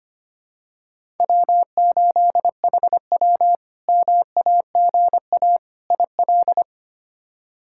Morse code, heard as a keyed tone, about 25 wpm, 700 Hz. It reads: W8HW MAGA IL
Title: K1USN SST QSO Practice - 25wpm